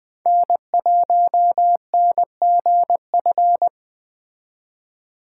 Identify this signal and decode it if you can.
Morse code, heard as a keyed tone, about 20 wpm, 700 Hz. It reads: N1NGF
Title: Speed Racing - USA 1x3 Call Signs 17wpm